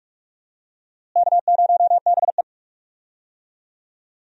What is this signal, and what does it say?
Morse code, heard as a keyed tone, about 42 wpm, 700 Hz. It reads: K0BE
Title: USA Call Signs - 4 Characters - Repeated 3x - 45wpm